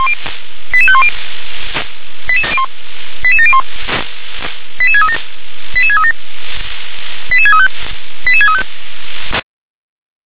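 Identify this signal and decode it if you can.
Five-tone selective call (ZVEI), heard as quick runs of five tones, tones of about 70 ms. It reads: 88510 88701 88637 88637 88635 88635
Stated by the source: Five-tone ZVEI selective call received on SDR.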